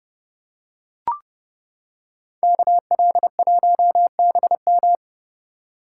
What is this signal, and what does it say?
Morse code, heard as a keyed tone, about 29 wpm, 700 Hz. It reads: KL1BM